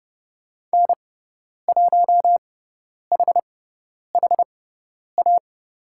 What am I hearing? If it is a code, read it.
Morse code, heard as a keyed tone, about 30 wpm, spaced as for 10 wpm, 700 Hz. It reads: N1HHA
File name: Lesson 16g - Letter c - Call Signs 30wpm